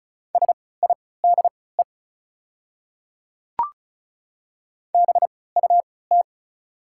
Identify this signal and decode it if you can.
Morse code, heard as a keyed tone, about 34 wpm, spaced as for 20 wpm, 690 Hz. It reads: SIDE BUT
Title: 3x Character Spacing - Top 100 Words - 35wpm